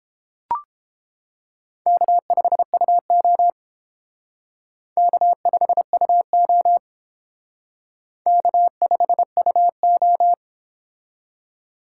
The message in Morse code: K5UO K5UO K5UO

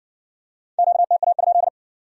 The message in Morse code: QTA?

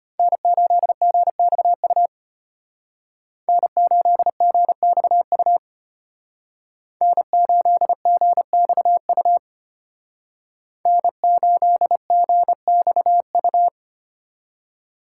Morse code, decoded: N8GXU N8GXU N8GXU N8GXU

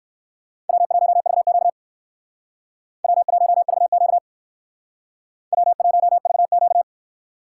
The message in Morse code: W1VQ W1VQ W1VQ

